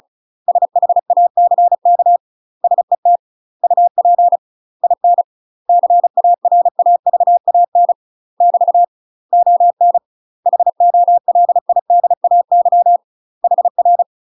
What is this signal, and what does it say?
Morse code, heard as a keyed tone, about 34 wpm, 700 Hz. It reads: SHACK SET UP IN CARAVAN <BT> ON HOLIDAY HR
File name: UK MASTER QSOs TRAINING / 35WPM / UK MASTER QSOs 0037